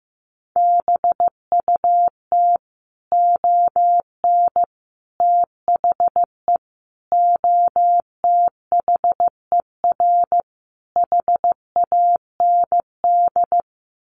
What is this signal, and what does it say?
Morse code, heard as a keyed tone, about 15 wpm, 700 Hz. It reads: BUT ON THE OTHER HAND
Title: Sets of 5 Words 15wpm